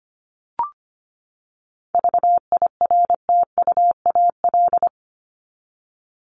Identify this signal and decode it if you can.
Morse code, heard as a keyed tone, about 25 wpm, 700 Hz. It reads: VIRTUAL
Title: LY Suffix - Top 100 Words with Base Words 25wpm